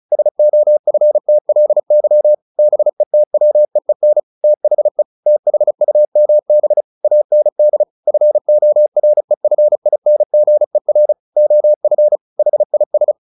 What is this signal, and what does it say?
Morse code, heard as a keyed tone, about 34 wpm, 600 Hz. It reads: SOFTLY BETWEEN THE THUMB AND FOREFINGER OF HIS
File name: the old man and the sea / the old man and the sea 1 35wpm